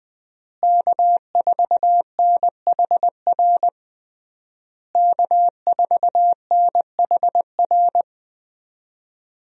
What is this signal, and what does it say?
Morse code, heard as a keyed tone, about 20 wpm, 700 Hz. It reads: K4NHR K4NHR
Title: USA Call Signs - 5 Characters - Repeated 3x - 20wpm